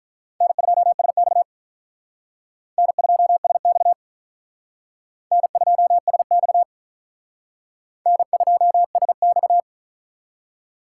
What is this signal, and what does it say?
Morse code, heard as a keyed tone, about 43 wpm, 700 Hz. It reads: N2SX N2SX N2SX N2SX
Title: Speed Racing - USA 1x2 Call Signs 35wpm